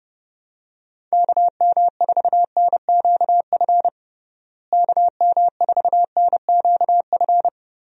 Morse code, heard as a keyed tone, about 31 wpm, 700 Hz. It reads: KM4NQF KM4NQF
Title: USA Call Signs - 6 Characters - Repeated 3x - 30wpm